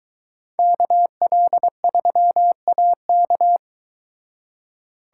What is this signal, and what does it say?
Morse code, heard as a keyed tone, about 22 wpm, 700 Hz. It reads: KL3AK